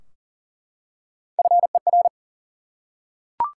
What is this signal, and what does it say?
Morse code, heard as a keyed tone, about 40 wpm, 700 Hz. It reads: FER
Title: Top 180 QSO Elements 40wpm